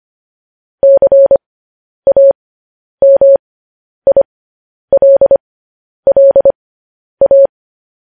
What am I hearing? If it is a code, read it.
Morse code, heard as a keyed tone, about 25 wpm, spaced as for 10 wpm, 550 Hz. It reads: CAMILLA
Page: news headlines in morse code at 10 wpm